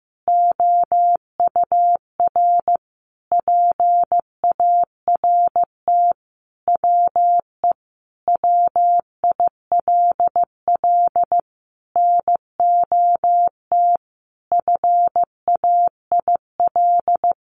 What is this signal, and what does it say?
Morse code, heard as a keyed tone, about 15 wpm, 700 Hz. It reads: OUR PART WE WILL NOT FAIL
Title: Sentences from Top 1100 Words 15wpm